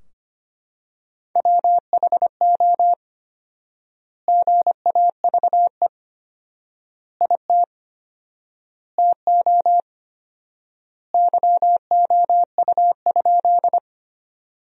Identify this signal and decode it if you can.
Morse code, heard as a keyed tone, about 25 wpm, 700 Hz. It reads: WHO GAVE IT TO YOU?
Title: 4x Word-Spacing - Sentences from Top 500 Words - 25wpm